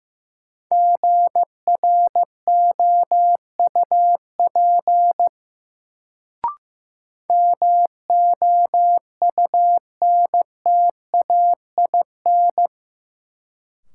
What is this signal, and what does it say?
Morse code, heard as a keyed tone, about 16 wpm, 700 Hz. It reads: GROUP MOUNTAIN
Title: Sentences from Top 1000 Words - Review Part 1 - New Word Drill - 15wpm